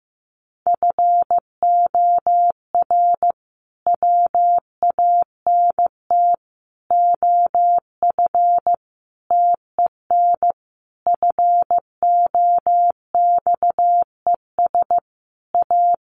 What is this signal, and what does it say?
Morse code, heard as a keyed tone, about 15 wpm, 700 Hz. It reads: FOR WANT OF TEN FOXES A
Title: A Sound Of Thunder 15wpm